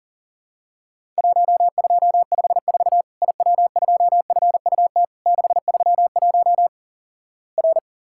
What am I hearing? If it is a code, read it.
Morse code, heard as a keyed tone, about 43 wpm, 700 Hz. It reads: 1254 IW2FUT 631 R
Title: Worked All Europe DX Contest - QTC Exchange - 40wpm